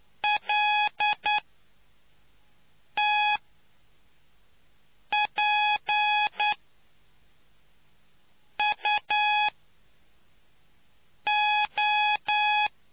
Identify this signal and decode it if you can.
Morse code, heard as a keyed tone, about 10 wpm, spaced as for 4 wpm, 840 Hz. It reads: LTPUO